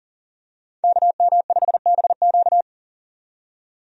Morse code, heard as a keyed tone, about 40 wpm, 700 Hz. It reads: KM5BQ
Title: USA Call Signs - 5 Characters - 40wpm